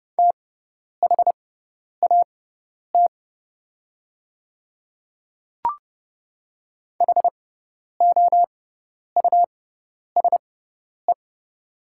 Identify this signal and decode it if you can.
Morse code, heard as a keyed tone, about 30 wpm, spaced as for 10 wpm, 700 Hz. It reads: THAT HOUSE
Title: 6x Character Spacing - Top 200 Words - 30wpm